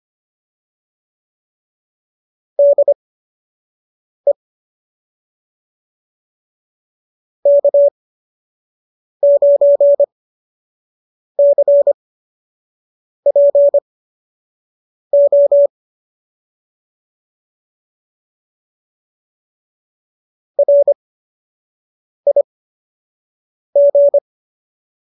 Morse code, heard as a keyed tone, about 25 wpm, spaced as for 6 wpm, 570 Hz. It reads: DE K9CPO RIG